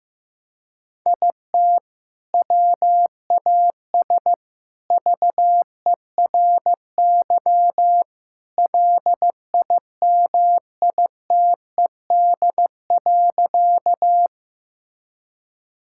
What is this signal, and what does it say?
Morse code, heard as a keyed tone, about 15 wpm, 700 Hz. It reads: IT WAS VERY LIMITED.
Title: Sentences from Top 1000 Words - Review Part 6 - 15wpm